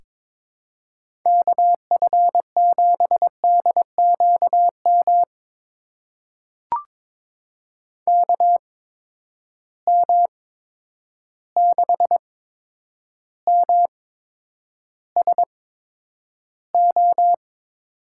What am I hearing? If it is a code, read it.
Morse code, heard as a keyed tone, about 22 wpm, spaced as for 9 wpm, 700 Hz. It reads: KF7DQM KM6MSO
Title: USA Call Signs - 6 Characters - 8x Character Spacing - 22wpm